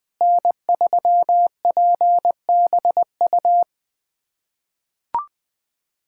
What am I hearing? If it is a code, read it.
Morse code, heard as a keyed tone, about 21 wpm, 700 Hz. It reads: N3PBU
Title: USA Callsign DE Callsign - 5 Characters - 20wpm